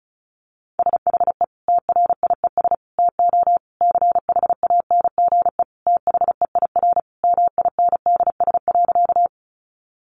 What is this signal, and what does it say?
Morse code, heard as a keyed tone, about 34 wpm, 700 Hz. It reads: SHE TRIES TO CHANGE THEIR MINDS.
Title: Easy Phrases 35wpm